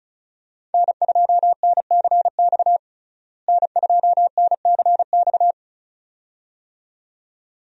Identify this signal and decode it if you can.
Morse code, heard as a keyed tone, about 36 wpm, 700 Hz. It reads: N2NCX N2NCX
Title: USA Call Signs - 5 Characters - Repeated 2x - 35wpm